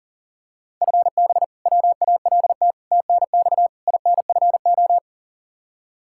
Morse code, heard as a keyed tone, about 40 wpm, 700 Hz. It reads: FB WALT TNX INFO
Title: Practice QSO - Exchange 3 - More Info - 40wpm